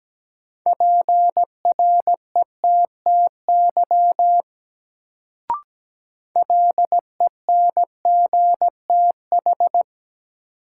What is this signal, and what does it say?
Morse code, heard as a keyed tone, about 17 wpm, 700 Hz. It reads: PRETTY LENGTH
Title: Sentences from Top 1000 Words - New Word Drill - 17wpm